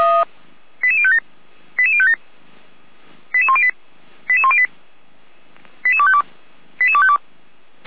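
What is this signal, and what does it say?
Five-tone selective call (ZVEI), heard as quick runs of five tones, tones of about 70 ms. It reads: 88057 88057 88108 88108 88252 88252
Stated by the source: ZVEI five-tone selective call received by SDR, from the feuerwehr-orlamuende.org (Feuerwehr Orlamünde) fire brigade archive.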